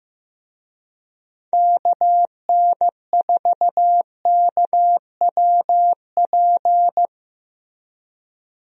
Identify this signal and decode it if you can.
Morse code, heard as a keyed tone, about 15 wpm, 700 Hz. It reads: KN4KWP
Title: USA Call Signs - 6 Characters - Repeated 2x - 15wpm